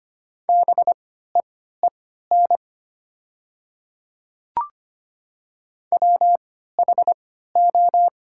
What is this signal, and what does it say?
Morse code, heard as a keyed tone, about 25 wpm, spaced as for 14 wpm, 700 Hz. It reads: BEEN WHO
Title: 3x Character Spacing - Top 100 Words - 25wpm